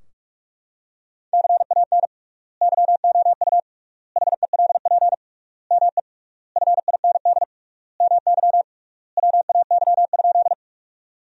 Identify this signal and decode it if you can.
Morse code, heard as a keyed tone, about 41 wpm, 700 Hz. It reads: CAN YOU HELP ME FIND MY WAY?
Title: 3x Word-Spacing - Sentences from Top 200 Words - 45wpm